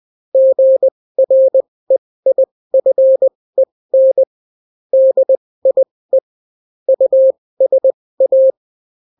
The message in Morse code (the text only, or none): GREIFEN DIE USA